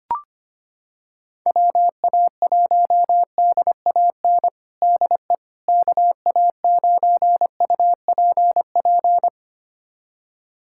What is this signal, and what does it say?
Morse code, heard as a keyed tone, about 25 wpm, 700 Hz. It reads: WA1DAN DE KA9UPP
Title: USA Callsign DE Callsign 25wpm